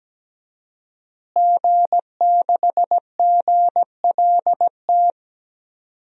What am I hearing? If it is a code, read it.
Morse code, heard as a keyed tone, about 17 wpm, 700 Hz. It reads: G6GLT
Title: DXCC-100 Call Signs - 5 Characters - Repeated 2x - 17wpm